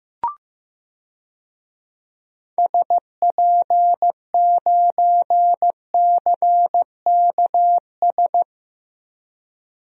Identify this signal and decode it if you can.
Morse code, heard as a keyed tone, about 15 wpm, 700 Hz. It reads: SP9CKS
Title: DXCC-100 Call Signs - 6 Characters - 15wpm